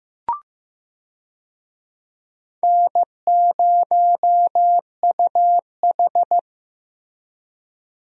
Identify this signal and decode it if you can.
Morse code, heard as a keyed tone, about 15 wpm, 700 Hz. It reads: N0UH
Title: USA Call Signs - 4 Characters - Repeated 2x - 15wpm